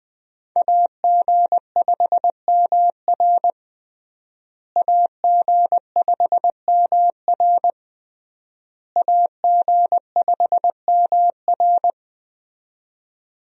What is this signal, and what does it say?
Morse code, heard as a keyed tone, about 19 wpm, 700 Hz. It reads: AG5MR AG5MR AG5MR
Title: USA Call Signs - 5 Characters - Repeated 3x - 20wpm